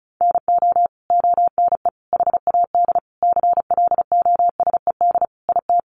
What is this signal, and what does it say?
Morse code, heard as a keyed tone, about 34 wpm, 700 Hz. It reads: NO ONE HAD CLOSED IT
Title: Sentences from Top 1400 Words 35wpm